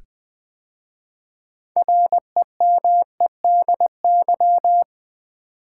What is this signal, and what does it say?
Morse code, heard as a keyed tone, about 20 wpm, 700 Hz. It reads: REMEDY
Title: 6 Letter Words - First 3 Letters Given - 20wpm